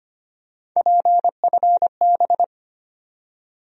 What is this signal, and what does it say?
Morse code, heard as a keyed tone, about 25 wpm, 700 Hz. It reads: PFB
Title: USA Call Sign Postfixes - 3 Letters 25wpm